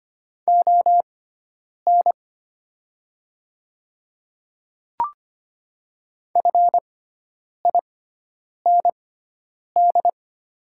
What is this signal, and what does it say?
Morse code, heard as a keyed tone, about 25 wpm, spaced as for 9 wpm, 700 Hz. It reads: ON FIND
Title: 6x Character Spacing - Top 100 Words - 25wpm